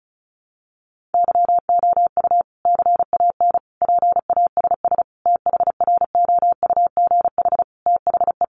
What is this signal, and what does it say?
Morse code, heard as a keyed tone, about 34 wpm, 700 Hz. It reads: YOU CAN PASS THROUGH THE
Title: Sentences from Top 400 Words 35wpm